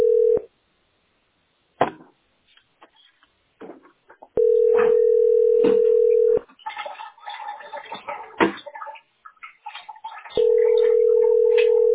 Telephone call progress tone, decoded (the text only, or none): ringback tone